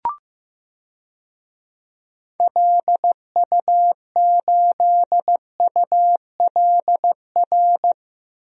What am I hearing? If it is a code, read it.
Morse code, heard as a keyed tone, about 15 wpm, 700 Hz. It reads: LU8ULR
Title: DXCC-100 Call Signs - 6 Characters - Repeated 3x - 15wpm